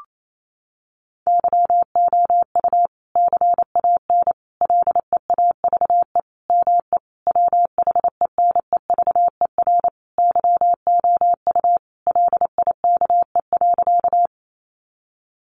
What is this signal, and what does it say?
Morse code, heard as a keyed tone, about 28 wpm, 700 Hz. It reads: YOU CAN LEAVE ME WHENEVER YOU LIKE.
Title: Sentences from Top 1200 Words 28wpm